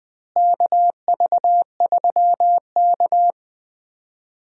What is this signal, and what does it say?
Morse code, heard as a keyed tone, about 20 wpm, 700 Hz. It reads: KV3K
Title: USA Call Signs - 4 Characters - Repeated 2x - 20wpm